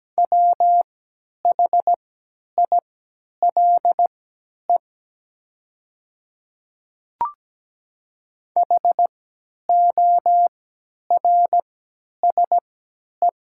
Morse code, heard as a keyed tone, about 17 wpm, spaced as for 10 wpm, 700 Hz. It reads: WHILE HORSE
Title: 3x Character Spacing - Top 300 Words - 17wpm